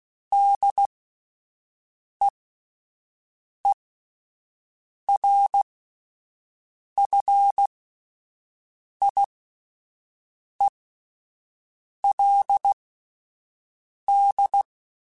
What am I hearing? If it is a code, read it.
Morse code, heard as a keyed tone, about 16 wpm, spaced as for 5 wpm, 800 Hz. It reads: DEERFIELD